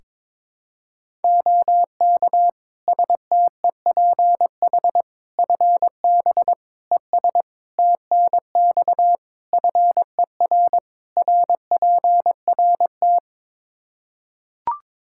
Morse code, heard as a keyed tone, about 22 wpm, 700 Hz. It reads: OK STEPH FB ES TNX FER RPRT
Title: Practice QSO - Exchange 2 - Info - 22wpm